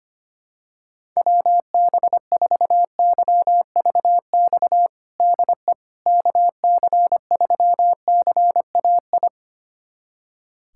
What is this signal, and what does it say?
Morse code, heard as a keyed tone, about 25 wpm, 700 Hz. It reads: WB4YVX DE KC3CAI
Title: USA Callsign DE Callsign - 6 Characters - 25wpm